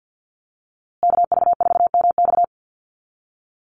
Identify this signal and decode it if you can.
Morse code, heard as a keyed tone, about 50 wpm, 700 Hz. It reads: KV4MX
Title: USA Call Signs Repeated 3x 50wpm